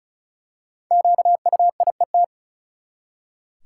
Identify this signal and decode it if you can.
Morse code, heard as a keyed tone, about 36 wpm, 700 Hz. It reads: QUIET
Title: Five Letter Words - 35wpm